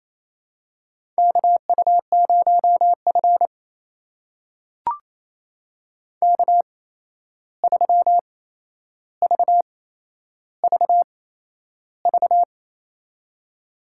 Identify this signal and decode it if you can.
Morse code, heard as a keyed tone, about 28 wpm, spaced as for 11 wpm, 700 Hz. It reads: KU0F K3VVV
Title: USA Call Signs - 8x Character Spacing - 28wpm